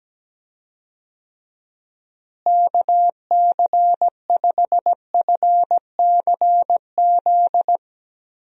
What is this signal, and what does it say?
Morse code, heard as a keyed tone, about 17 wpm, 700 Hz. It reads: KC5FCZ
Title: USA Call Signs - 6 Characters - Repeated 2x - 17wpm